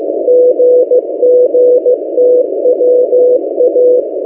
Morse code, heard as a keyed tone, about 15 wpm, 500 Hz. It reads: GGTWA
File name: M08a- -1355utc-8096khz